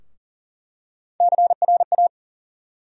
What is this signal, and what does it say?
Morse code, heard as a keyed tone, about 40 wpm, 700 Hz. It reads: CRA
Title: USA Call Sign Postfixes - 3 Letters 40wpm